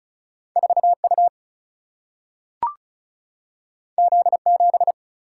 Morse code, heard as a keyed tone, about 36 wpm, 700 Hz. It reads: 4U Z7
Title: Single-Letter Single-Number 35wpm